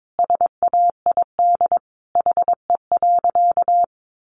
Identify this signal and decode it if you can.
Morse code, heard as a keyed tone, about 22 wpm, 700 Hz. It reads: SAID HE.
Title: Sentences from Top 400 Words 22wpm